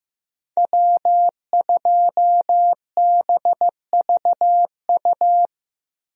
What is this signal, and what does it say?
Morse code, heard as a keyed tone, about 16 wpm, 700 Hz. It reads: W2BVU
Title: USA Call Signs - 5 Characters - Repeated 2x - 15wpm